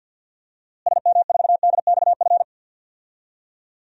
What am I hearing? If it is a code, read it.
Morse code, heard as a keyed tone, about 53 wpm, 700 Hz. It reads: SM4DXF
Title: DXCC-100 Call Signs - 6 Characters - Repeated 2x - 50wpm